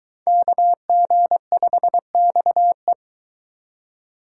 Morse code, heard as a keyed tone, about 20 wpm, 700 Hz. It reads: KG5XE